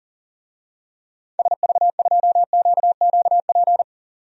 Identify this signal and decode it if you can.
Morse code, heard as a keyed tone, about 40 wpm, 700 Hz. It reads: SV2QQP
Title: DXCC-100 Call Signs - 6 Characters - 40wpm